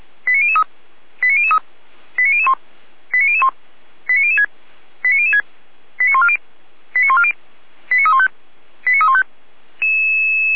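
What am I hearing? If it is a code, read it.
Five-tone selective call (ZVEI), heard as quick runs of five tones, tones of about 70 ms. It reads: 89003 89003 89001 89001 89006 89006 89140 89140 89315 89315
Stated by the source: ZVEI five-tone selective call 89099 received on SDR, logged by the Feuerwehr Orlamünde fire brigade server.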